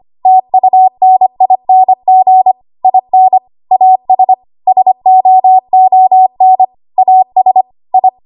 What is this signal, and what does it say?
Morse code, heard as a keyed tone, about 25 wpm, 750 Hz. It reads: TUNING IN AS SOON AS I
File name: Archive / 25 WPM / 25WPM